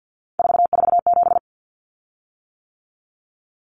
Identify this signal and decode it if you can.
Morse code, heard as a keyed tone, about 50 wpm, 700 Hz. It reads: 447
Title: Common Cut Numbers - 3-Digit 50wpm